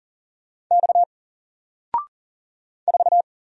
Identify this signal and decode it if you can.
Morse code, heard as a keyed tone, about 40 wpm, 700 Hz. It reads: X 4
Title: Single Letter-Number 40wpm